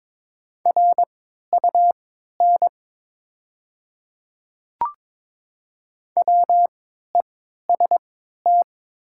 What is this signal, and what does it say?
Morse code, heard as a keyed tone, about 22 wpm, spaced as for 13 wpm, 700 Hz. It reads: RUN WEST
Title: 3x Character Spacing - Top 500 Words - 22wpm